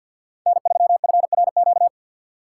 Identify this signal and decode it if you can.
Morse code, heard as a keyed tone, about 47 wpm, 690 Hz. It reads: N3FRQ
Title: USA Call Signs - 5 Characters - 3x Character Spacing - 50wpm